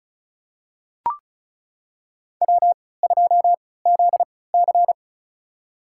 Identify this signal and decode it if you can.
Morse code, heard as a keyed tone, about 36 wpm, spaced as for 20 wpm, 690 Hz. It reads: W2ZC
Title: USA Call Signs - 4 Characters - 3x Character Spacing - 35wpm